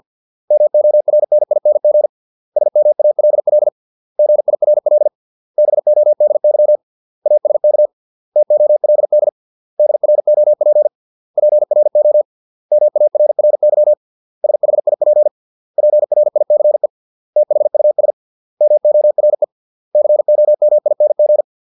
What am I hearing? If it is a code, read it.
Morse code, heard as a keyed tone, about 50 wpm, 600 Hz. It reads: MORNING SMALL GIRL BODY ASK TOLD DROP PRO MARRY SHIP PRICE THUS MORE COMING